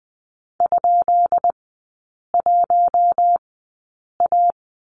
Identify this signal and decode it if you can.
Morse code, heard as a keyed tone, about 20 wpm, spaced as for 8 wpm, 700 Hz. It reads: ?1A
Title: 178 Lesson 29e - Slash - Three Characters 20wpm